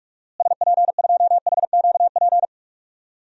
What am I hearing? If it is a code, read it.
Morse code, heard as a keyed tone, about 41 wpm, 700 Hz. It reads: SP2HQP
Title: DXCC-100 Call Signs - 6 Characters - Repeated 3x - 45wpm